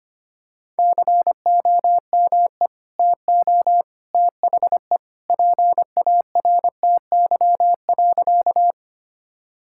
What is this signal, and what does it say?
Morse code, heard as a keyed tone, about 25 wpm, 700 Hz. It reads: COME TO THE PARTY.
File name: practice / 25wpm / Sentences with 2-Word Phrases - 25wpm